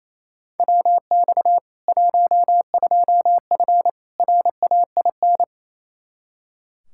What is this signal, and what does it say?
Morse code, heard as a keyed tone, about 28 wpm, 700 Hz. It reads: WX 12F RAIN
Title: K1USN SST QSO Practice 28wpm